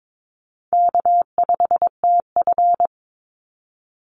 Morse code, heard as a keyed tone, about 22 wpm, 700 Hz. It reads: K5TF
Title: Call Signs Repeated 2x - 22wpm